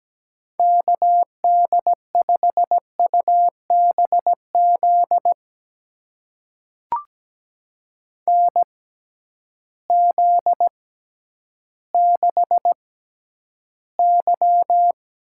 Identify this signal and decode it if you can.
Morse code, heard as a keyed tone, about 17 wpm, spaced as for 10 wpm, 700 Hz. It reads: KD5UBZ NZ6Y
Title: USA Call Signs - 6x Character Spacing - 17wpm